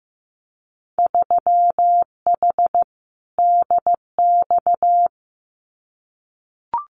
Morse code, heard as a keyed tone, about 15 wpm, 700 Hz. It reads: 3H DX
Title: Winter Field Day Exchanges 15wpm